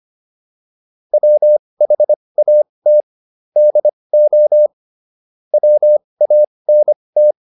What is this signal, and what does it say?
Morse code, heard as a keyed tone, about 25 wpm, spaced as for 20 wpm, 600 Hz. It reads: WHAT DO WANT